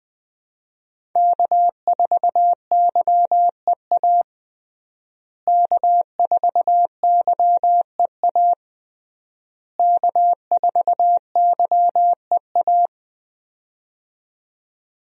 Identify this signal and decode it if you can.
Morse code, heard as a keyed tone, about 20 wpm, 700 Hz. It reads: K4YEA K4YEA K4YEA
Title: USA Call Signs - 5 Characters - Repeated 3x - 20wpm